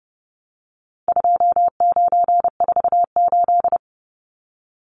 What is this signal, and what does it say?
Morse code, heard as a keyed tone, about 30 wpm, 700 Hz. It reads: 2948
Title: Numbers - 4-Digit 30wpm